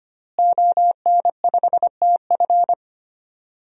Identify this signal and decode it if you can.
Morse code, heard as a keyed tone, about 25 wpm, 700 Hz. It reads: ON5TF
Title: DXCC-100 Call Signs - 5 Characters - Repeated 2x - 25wpm